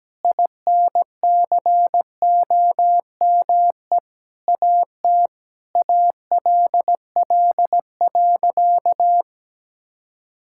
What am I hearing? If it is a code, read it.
Morse code, heard as a keyed tone, about 17 wpm, 700 Hz. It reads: INCOME AT ALL.
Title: Sentences from Top 1000 Words - Review Part 5 - 17wpm